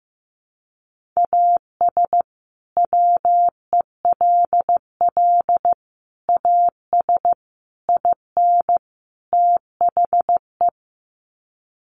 Morse code, heard as a keyed tone, about 15 wpm, 700 Hz. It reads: AS WELL AS IN THE
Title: Sets of 5 Words Encore 15wpm